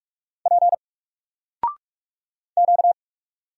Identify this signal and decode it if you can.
Morse code, heard as a keyed tone, about 44 wpm, 700 Hz. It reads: P Q
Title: Single Character (Letter:Number:Punctuation:Prosign) 45wpm